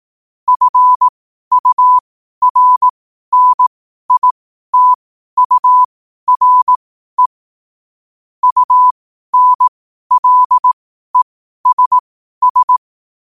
Morse code, heard as a keyed tone, about 18 wpm, spaced as for 12 wpm, 1000 Hz. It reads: FURNITURE UNLESS
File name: qotdfiles / 12wpm